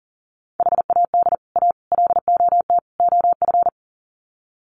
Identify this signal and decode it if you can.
Morse code, heard as a keyed tone, about 40 wpm, 700 Hz. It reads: HAD A LOT OF
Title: Sets of 4 Words 40wpm